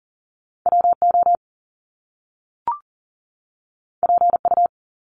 Morse code, heard as a keyed tone, about 42 wpm, 700 Hz. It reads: WO PU
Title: Top Two Letter Combinations in Words 40wpm